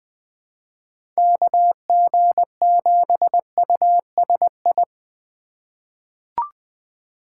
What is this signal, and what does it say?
Morse code, heard as a keyed tone, about 20 wpm, 700 Hz. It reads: KG7USI
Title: USA Call Signs - 6 Characters - 2x Character Spacing - 20wpm